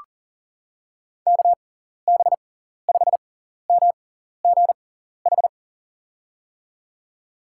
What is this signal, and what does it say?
Morse code, heard as a keyed tone, about 40 wpm, spaced as for 14 wpm, 700 Hz. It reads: KB5MGH
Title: USA Call Signs - 6 Characters - 6x Character Spacing - 40wpm